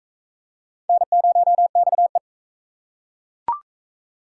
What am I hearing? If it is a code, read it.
Morse code, heard as a keyed tone, about 42 wpm, 700 Hz. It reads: N0XE